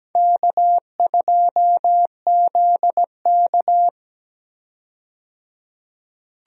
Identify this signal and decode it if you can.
Morse code, heard as a keyed tone, about 17 wpm, 700 Hz. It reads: K2ZK
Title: USA Call Signs - 4 Characters - 17wpm